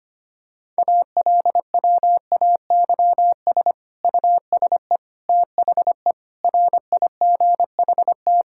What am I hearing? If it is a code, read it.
Morse code, heard as a keyed tone, about 25 wpm, 700 Hz. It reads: ALWAYS USE THE RIGHT
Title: Sentences from Top 1000 Words - Review Part 6 - 25wpm